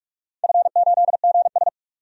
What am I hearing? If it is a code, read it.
Morse code, heard as a keyed tone, about 46 wpm, 700 Hz. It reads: F8GS
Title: DXCC-100 Call Signs - 4 Characters - 45wpm